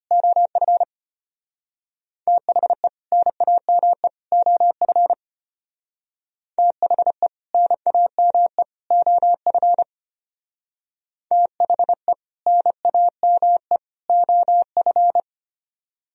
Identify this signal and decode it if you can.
Morse code, heard as a keyed tone, about 32 wpm, 700 Hz. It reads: OF THE NAME OF THE NAME OF THE NAME OF